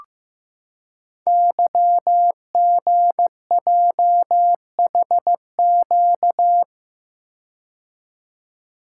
Morse code, heard as a keyed tone, about 15 wpm, 700 Hz. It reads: YGJHQ
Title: Code Groups - 5 Letters - 15wpm